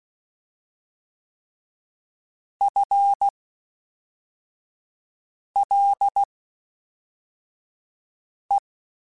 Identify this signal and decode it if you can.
Morse code, heard as a keyed tone, about 16 wpm, spaced as for 4 wpm, 800 Hz. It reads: FLE
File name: Spaced / FL Runs sp mp3s / FL Runs sp @5